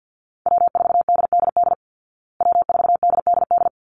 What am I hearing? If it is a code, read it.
Morse code, heard as a keyed tone, about 50 wpm, 700 Hz. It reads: W4DDD W4DDD
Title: Call Signs Repeated 3x - 50wpm